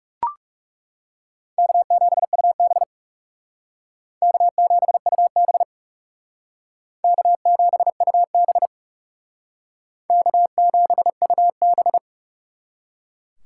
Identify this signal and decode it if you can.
Morse code, heard as a keyed tone, about 38 wpm, 700 Hz. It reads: K7UB K7UB K7UB K7UB